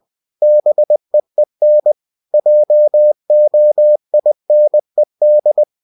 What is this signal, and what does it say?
Morse code, heard as a keyed tone, about 20 wpm, 600 Hz. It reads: BEEN JOINED
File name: cw-books / winnie-the-pooh / winnie-20wpm-page-0022